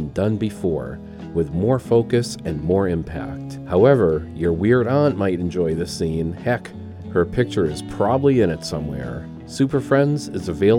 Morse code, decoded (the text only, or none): none